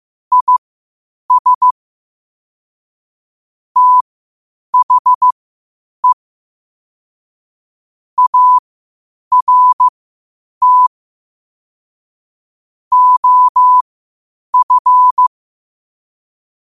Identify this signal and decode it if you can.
Morse code, heard as a keyed tone, about 15 wpm, spaced as for 8 wpm, 1000 Hz. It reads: IS THE ART OF
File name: qotdfiles / 8wpm